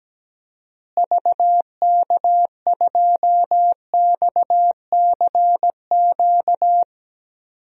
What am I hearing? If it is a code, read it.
Morse code, heard as a keyed tone, about 17 wpm, 700 Hz. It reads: VK2XCQ
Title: DXCC-100 Call Signs - 6 Characters - Repeated 3x - 17wpm